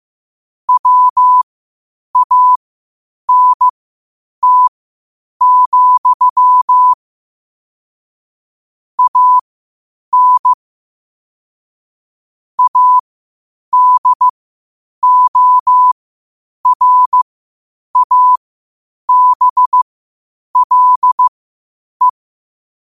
Morse code, heard as a keyed tone, about 15 wpm, spaced as for 8 wpm, 1000 Hz. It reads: WANT, AN ADORABLE
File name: qotdfiles / 8wpm